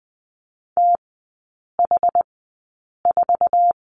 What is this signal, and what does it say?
Morse code, heard as a keyed tone, about 20 wpm, spaced as for 8 wpm, 700 Hz. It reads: TH4
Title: 060 Lesson 11e - Letter h - Three Characters 20wpm